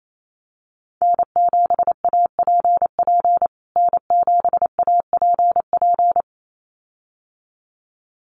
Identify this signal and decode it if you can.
Morse code, heard as a keyed tone, about 28 wpm, 700 Hz. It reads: N7APP N7APP
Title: USA Call Signs Repeated 2x 28wpm